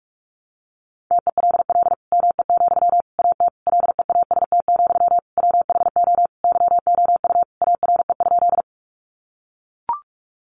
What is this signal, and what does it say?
Morse code, heard as a keyed tone, about 44 wpm, 700 Hz. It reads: TELL ME, AT LEAST, WHO YOU ARE?
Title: Sentences from Top 700 Words 45wpm